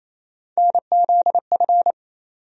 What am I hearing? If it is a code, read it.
Morse code, heard as a keyed tone, about 28 wpm, 700 Hz. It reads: NZF
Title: USA Call Sign Postfixes - 3 Letters 28wpm